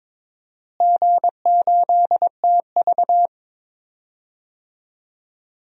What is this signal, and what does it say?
Morse code, heard as a keyed tone, about 21 wpm, 700 Hz. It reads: G8TV
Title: DXCC-100 Call Signs - 4 Characters - Repeated 2x - 22wpm